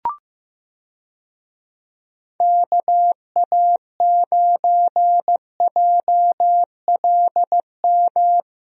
Morse code, heard as a keyed tone, about 15 wpm, 700 Hz. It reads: KA9JLM
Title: USA Call Signs - 6 Characters - Repeated 2x - 15wpm